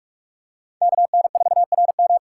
Morse code, heard as a keyed tone, about 46 wpm, 700 Hz. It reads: KN4RM
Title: USA Call Signs - 5 Characters - 6x Character Spacing - 45wpm